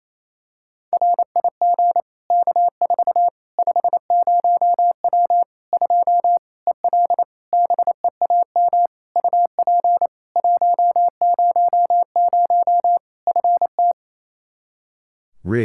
Morse code, heard as a keyed tone, about 27 wpm, 700 Hz. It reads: RIG K4 50W 2 EL BEAM UP 100 FT
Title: Practice QSO - Exchange 2 - Info - 28wpm